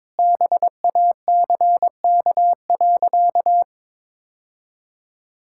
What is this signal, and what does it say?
Morse code, heard as a keyed tone, about 22 wpm, 700 Hz. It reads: BACK.
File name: practice / 22wpm / Sentences with 2-Word Phrases - 22wpm